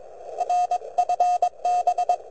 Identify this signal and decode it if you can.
Morse code, heard as a keyed tone, about 22 wpm, 720 Hz. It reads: R FB
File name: vq9la / CX5BW 80MTR CW 01SEP2008 0114Z